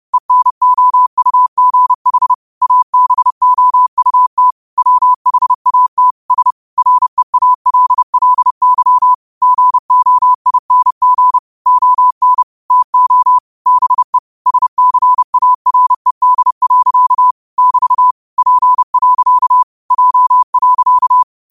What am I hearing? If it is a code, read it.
Morse code, heard as a keyed tone, about 30 wpm, 1000 Hz. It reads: ENOUGH ABOUT WHAT S REALLY GOING ON TO BE SCARED. <BT> P. J.
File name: qotdfiles / 30wpm